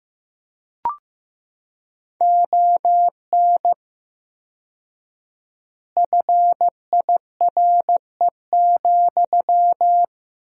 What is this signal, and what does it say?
Morse code, heard as a keyed tone, about 15 wpm, 700 Hz. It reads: ON FIRE,
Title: 4x Word-Spacing - Sentences from Top 400 Words - 15wpm